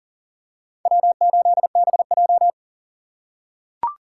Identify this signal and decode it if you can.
Morse code, heard as a keyed tone, about 39 wpm, 700 Hz. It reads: W8BJ
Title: USA Call Signs - 4 Characters - 3x Character Spacing - 40wpm